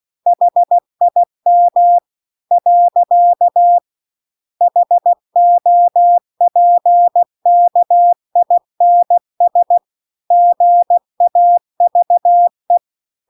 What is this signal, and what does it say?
Morse code, heard as a keyed tone, about 16 wpm, 700 Hz. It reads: HIM . HOPKINS GAVE